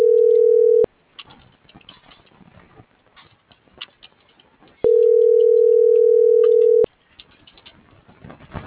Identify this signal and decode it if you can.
Ringback tone (North America), 440 + 480 Hz.